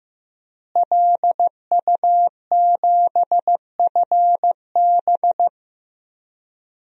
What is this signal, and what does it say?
Morse code, heard as a keyed tone, about 15 wpm, 700 Hz. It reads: LU7FB
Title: DXCC-100 Call Signs - 5 Characters - Repeated 3x - 15wpm